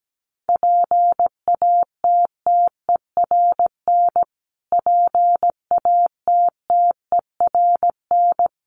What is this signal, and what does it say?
Morse code, heard as a keyed tone, about 17 wpm, 700 Hz. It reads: PATTERN PATTERN
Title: Top 500 Words Repeated 17wpm